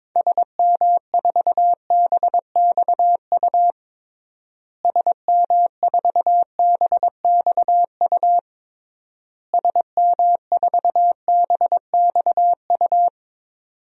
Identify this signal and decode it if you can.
Morse code, heard as a keyed tone, about 22 wpm, 700 Hz. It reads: SM4BXU SM4BXU SM4BXU
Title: DXCC-100 Call Signs - 6 Characters - Repeated 3x - 22wpm